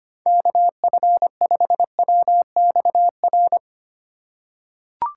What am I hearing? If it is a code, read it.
Morse code, heard as a keyed tone, about 25 wpm, 700 Hz. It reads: KF5WXR
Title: USA Call Signs - 3x Character Spacing - 25wpm